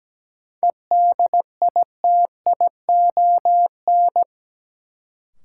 Morse code, heard as a keyed tone, about 17 wpm, 700 Hz. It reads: EDITION
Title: Sets of 5 Words - New Word Drill - 17wpm